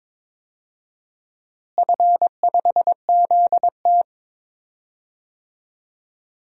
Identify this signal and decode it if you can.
Morse code, heard as a keyed tone, about 22 wpm, 700 Hz. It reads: F5ZT
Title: DXCC-100 Call Signs - 4 Characters - Repeated 3x - 22wpm